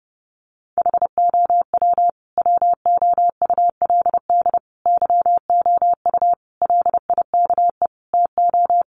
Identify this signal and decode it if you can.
Morse code, heard as a keyed tone, about 29 wpm, 700 Hz. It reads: HOW WOULD YOU LIKE TO
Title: Sentences from Top 100 Words 30wpm